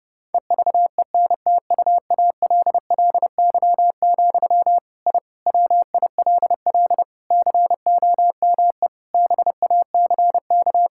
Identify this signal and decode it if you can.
Morse code, heard as a keyed tone, about 30 wpm, 700 Hz. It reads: EVENTUALLY, I WILL COME BACK
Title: Sentences from Top 1000 Words - Review Part 6 - 30wpm